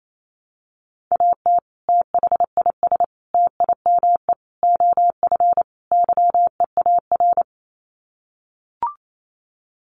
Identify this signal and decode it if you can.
Morse code, heard as a keyed tone, about 28 wpm, 700 Hz. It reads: AT THIS TIME OF YEAR
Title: Sets of 5 Words Encore2 28wpm